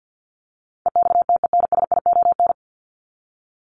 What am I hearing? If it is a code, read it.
Morse code, heard as a keyed tone, about 50 wpm, 700 Hz. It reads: EXTENSION
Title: Top 1501-1750 Words 50wpm